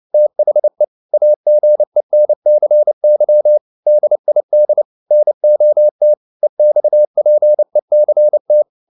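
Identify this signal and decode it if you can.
Morse code, heard as a keyed tone, about 29 wpm, 600 Hz. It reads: THE AGENCY DID NOT EXPECT
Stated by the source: Morse code at 29 wpm, sending Ken Follett's novel Never in English, from CW med Gnister.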